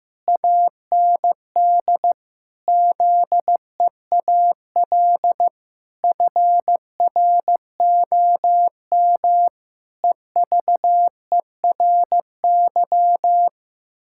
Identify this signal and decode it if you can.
Morse code, heard as a keyed tone, about 16 wpm, 700 Hz. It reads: AND ZEAL FROM EVERY
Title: Pangram Challenge v2 - 15wpm